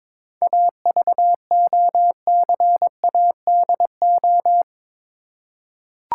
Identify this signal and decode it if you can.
Morse code, heard as a keyed tone, about 22 wpm, 700 Hz. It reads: AVOCADO
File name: Difficult Words with Hints - 22wpm